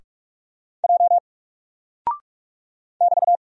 Morse code, heard as a keyed tone, about 47 wpm, 700 Hz. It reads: J <BT>
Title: Single Character (Letter:Number:Punctuation:Prosign) Alt 45wpm